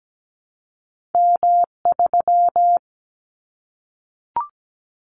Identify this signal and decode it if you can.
Morse code, heard as a keyed tone, about 17 wpm, 700 Hz. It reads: M3
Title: Callsign Prefixes for Top 25 DXCC - Repeated 3x - 17wpm